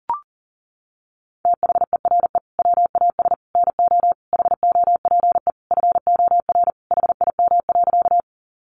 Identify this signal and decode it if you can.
Morse code, heard as a keyed tone, about 40 wpm, 700 Hz. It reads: THERE WAS NO HOPE FOR HIM.
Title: Sentences from Top 1000 Words 40wpm